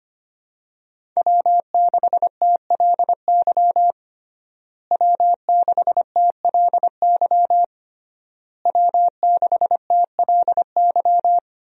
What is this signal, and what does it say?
Morse code, heard as a keyed tone, about 25 wpm, 700 Hz. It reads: W6TLY W6TLY W6TLY
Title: USA Call Signs - 5 Characters - Repeated 3x - 25wpm